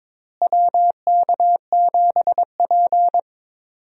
Morse code, heard as a keyed tone, about 22 wpm, 700 Hz. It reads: WK7P
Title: USA Call Signs - 4 Characters - Repeated 2x - 22wpm